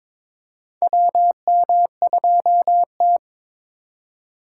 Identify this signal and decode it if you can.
Morse code, heard as a keyed tone, about 22 wpm, 700 Hz. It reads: WM2T
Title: USA Call Signs - 4 Characters - 4x Character Spacing - 22wpm